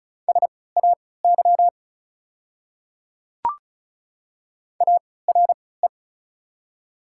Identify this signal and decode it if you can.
Morse code, heard as a keyed tone, about 34 wpm, spaced as for 20 wpm, 690 Hz. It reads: SAY ARE
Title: 3x Character Spacing - Top 500 Words - 35wpm